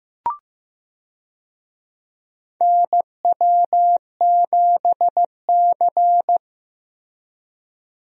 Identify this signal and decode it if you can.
Morse code, heard as a keyed tone, about 15 wpm, 700 Hz. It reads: NW7C
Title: USA Call Signs - 4 Characters - Repeated 3x - 15wpm